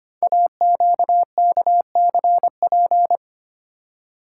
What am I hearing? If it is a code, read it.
Morse code, heard as a keyed tone, about 25 wpm, 700 Hz. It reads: AQKCP
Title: Code Groups - 5 Letters - 25wpm